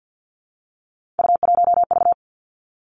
Morse code, heard as a keyed tone, about 50 wpm, 700 Hz. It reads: U1V
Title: Three Challenging Letter-Number Combinations 50wpm